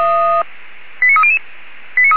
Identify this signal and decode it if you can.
Five-tone selective call (ZVEI), heard as quick runs of five tones, tones of about 70 ms. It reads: 89300